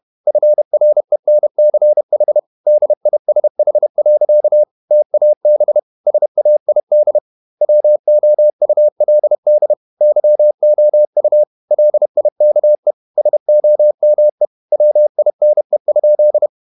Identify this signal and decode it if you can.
Morse code, heard as a keyed tone, about 31 wpm, 600 Hz. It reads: FRENCH DISH. TAB SAID WOULD YOU LIKE SOME WINE?
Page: Never 31wpm | CW med Gnister